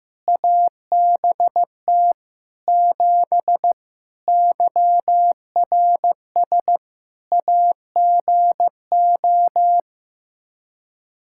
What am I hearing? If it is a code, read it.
Morse code, heard as a keyed tone, about 16 wpm, 700 Hz. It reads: ABT 7 YRS AGO
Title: Practice QSO - Exchange 3 - More Info - 15wpm